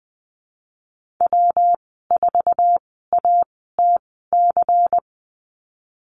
Morse code, heard as a keyed tone, about 20 wpm, spaced as for 15 wpm, 700 Hz. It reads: W4ATC
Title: 140 Lesson 23g - Letter f - Call Signs 20wpm